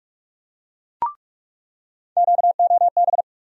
Morse code, heard as a keyed tone, about 46 wpm, 700 Hz. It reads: QOB
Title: USA Call Sign Postfixes - 3 Letters 45wpm